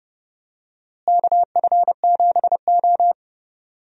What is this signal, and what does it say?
Morse code, heard as a keyed tone, about 30 wpm, 700 Hz. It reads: KF7O
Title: USA Call Signs - 4 Characters - Repeated 3x - 30wpm